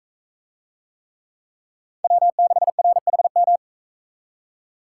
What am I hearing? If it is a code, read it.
Morse code, heard as a keyed tone, about 42 wpm, 700 Hz. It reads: W6RHM